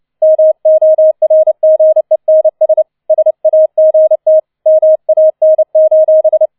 Morse code, heard as a keyed tone, about 29 wpm, 630 Hz. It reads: MORGENS SAGT MAN: